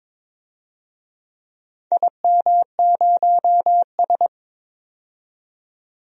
Morse code, heard as a keyed tone, about 22 wpm, 700 Hz. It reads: IM0S
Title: DXCC-100 Call Signs - 4 Characters - 22wpm